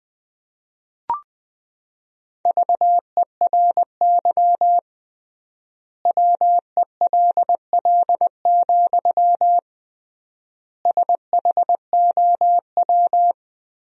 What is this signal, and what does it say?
Morse code, heard as a keyed tone, about 21 wpm, 700 Hz. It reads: VERY WELL, SHOW
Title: 3x Word-Spacing - Sentences from Top 200 Words - 20wpm